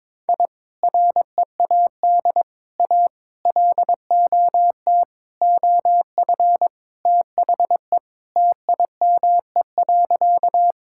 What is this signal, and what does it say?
Morse code, heard as a keyed tone, about 22 wpm, 700 Hz. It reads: I READ A LOT OF THE TIME.
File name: practice / 22wpm / Sentences with 4-Word Phrases - 22wpm